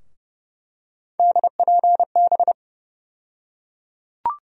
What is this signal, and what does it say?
Morse code, heard as a keyed tone, about 30 wpm, 700 Hz. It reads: DPB
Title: USA Call Sign Postfixes - 3 Letters 30wpm